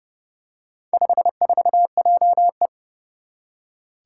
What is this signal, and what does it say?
Morse code, heard as a keyed tone, about 32 wpm, 700 Hz. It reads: 54JE
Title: Four Challenging Letter-Number Combinations 30wpm